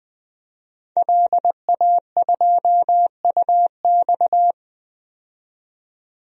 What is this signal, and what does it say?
Morse code, heard as a keyed tone, about 21 wpm, 700 Hz. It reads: LA2UX
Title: DXCC-100 Call Signs - 5 Characters - 20wpm